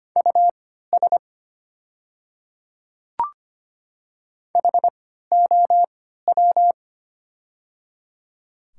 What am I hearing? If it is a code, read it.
Morse code, heard as a keyed tone, about 25 wpm, spaced as for 14 wpm, 690 Hz. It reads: US HOW
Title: 3x Character Spacing - Top 200 Words - 25wpm